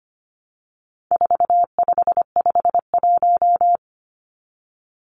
Morse code, heard as a keyed tone, about 25 wpm, 700 Hz. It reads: 4551
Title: Numbers - 4-Digit 25wpm